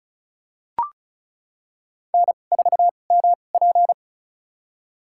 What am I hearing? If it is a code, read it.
Morse code, heard as a keyed tone, about 34 wpm, spaced as for 25 wpm, 700 Hz. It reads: N4MP
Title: USA Call Signs - 4 Characters - 2x Character Spacing - 35wpm